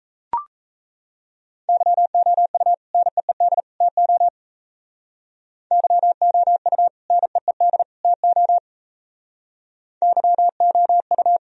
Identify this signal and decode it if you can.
Morse code, heard as a keyed tone, about 35 wpm, 700 Hz. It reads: YOU NEED TO YOU NEED TO YOU
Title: Speed-Racing - Sets of 3 Words 28wpm